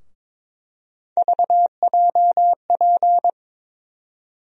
Morse code, heard as a keyed tone, about 22 wpm, 700 Hz. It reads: VJP
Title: USA Call Sign Postfixes - 3 Letters 22wpm